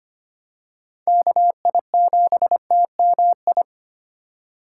Morse code, heard as a keyed tone, about 25 wpm, 700 Hz. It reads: KI7TMI
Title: USA Call Signs - 6 Characters - 4x Character Spacing - 25wpm